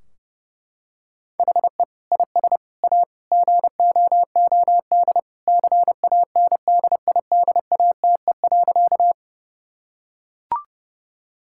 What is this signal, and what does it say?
Morse code, heard as a keyed tone, about 30 wpm, 700 Hz. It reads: HE IS A GOOD CANDIDATE.
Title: Sentences from Top 1600 Words 30wpm